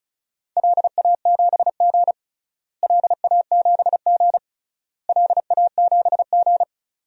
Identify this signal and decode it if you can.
Morse code, heard as a keyed tone, about 36 wpm, 700 Hz. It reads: LA7G LA7G LA7G
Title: DXCC-100 Call Signs - 4 Characters - Repeated 3x - 35wpm